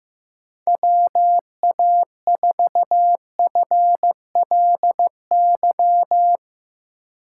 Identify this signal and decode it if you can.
Morse code, heard as a keyed tone, about 15 wpm, 700 Hz. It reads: WA4FLY
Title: USA Call Signs - 2x Character Spacing - 15wpm